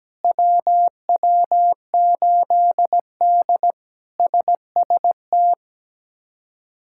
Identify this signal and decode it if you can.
Morse code, heard as a keyed tone, about 17 wpm, 700 Hz. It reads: WW8D SST
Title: K1USN SST QSO Practice - 17wpm